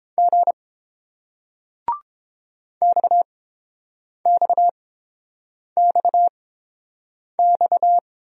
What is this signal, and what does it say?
Morse code, heard as a keyed tone, about 30 wpm, spaced as for 7 wpm, 690 Hz. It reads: G XXXX